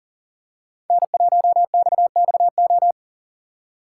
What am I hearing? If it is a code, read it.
Morse code, heard as a keyed tone, about 40 wpm, 700 Hz. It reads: N1XXO
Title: USA Call Signs - 5 Characters - 3x Character Spacing - 40wpm